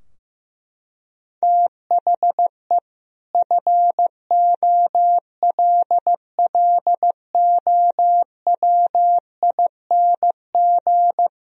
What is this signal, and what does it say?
Morse code, heard as a keyed tone, about 15 wpm, 700 Hz. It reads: THE FOLLOWING
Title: Sentences from Top 1000 Words - Review Part 4 - 15wpm